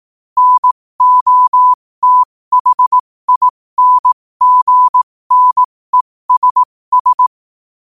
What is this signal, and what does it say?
Morse code, heard as a keyed tone, about 18 wpm, spaced as for 15 wpm, 1000 Hz. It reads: NOTHINGNESS
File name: qotdfiles / 15wpm